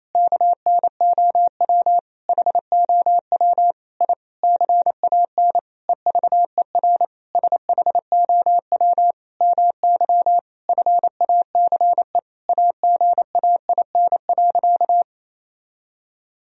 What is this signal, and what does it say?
Morse code, heard as a keyed tone, about 28 wpm, 700 Hz. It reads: KNOW HOW I CAN EVER SHOW MY FACE AGAIN.
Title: Sentences from Top 1000 Words - Review Part 2 - 28wpm